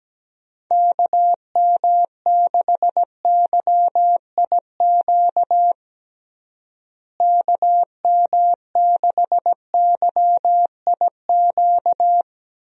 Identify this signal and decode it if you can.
Morse code, heard as a keyed tone, about 17 wpm, 700 Hz. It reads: KM6YIQ KM6YIQ
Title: USA Call Signs - 6 Characters - Repeated 2x - 17wpm